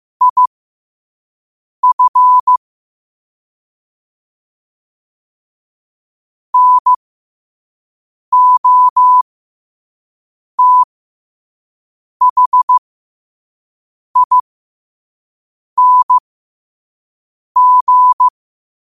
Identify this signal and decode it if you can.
Morse code, heard as a keyed tone, about 15 wpm, spaced as for 5 wpm, 1000 Hz. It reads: IF NOTHING
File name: qotdfiles / 5wpm